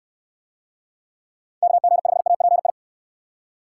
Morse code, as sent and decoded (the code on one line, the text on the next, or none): -.- --. ..... .- .--. ..
KG5API